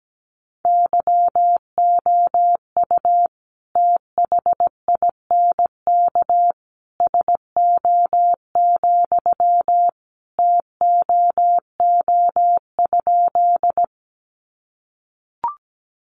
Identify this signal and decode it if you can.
Morse code, heard as a keyed tone, about 17 wpm, 700 Hz. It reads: YOU THINK SO, TOO?
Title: Sentences from Top 200 Words 17wpm